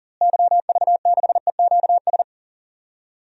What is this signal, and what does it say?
Morse code, heard as a keyed tone, about 40 wpm, 700 Hz. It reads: YV6EQS
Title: DXCC-100 Call Signs - 6 Characters - Repeated 3x - 40wpm